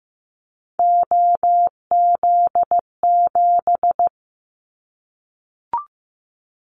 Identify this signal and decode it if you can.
Morse code, heard as a keyed tone, about 15 wpm, 700 Hz. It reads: OZ7
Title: Callsign Prefixes for Top 100 DXCC - 15wpm